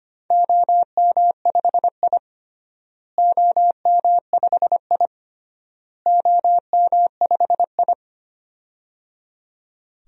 Morse code, heard as a keyed tone, about 25 wpm, 700 Hz. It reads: OM5I OM5I OM5I
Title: DXCC-100 Call Signs - 4 Characters - Repeated 3x - 25wpm